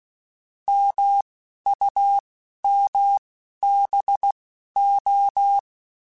Morse code, heard as a keyed tone, about 16 wpm, spaced as for 11 wpm, 800 Hz. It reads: MUMBO